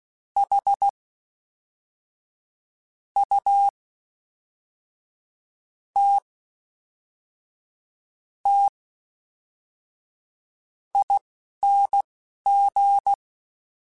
Morse code, heard as a keyed tone, about 16 wpm, spaced as for 5 wpm, 800 Hz. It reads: HUTTING
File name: Special INGs / Special INGs-mp3s / Fast ing @10